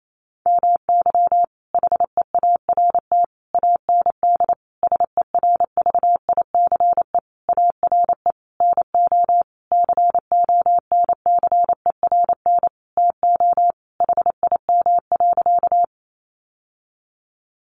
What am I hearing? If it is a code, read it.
Morse code, heard as a keyed tone, about 28 wpm, 700 Hz. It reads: MY HEART AND SERVICE ARE NO CONCERN TO HIM.
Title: Sentences from Top 1200 Words 28wpm